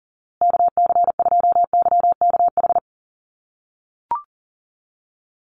Morse code, heard as a keyed tone, about 40 wpm, 700 Hz. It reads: KC2YKH
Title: USA Call Signs Repeated 2x 40wpm